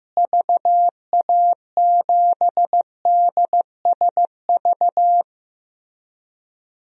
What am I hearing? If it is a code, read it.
Morse code, heard as a keyed tone, about 15 wpm, 700 Hz. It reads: VA7DSV
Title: DXCC-100 Call Signs - 6 Characters - Repeated 3x - 15wpm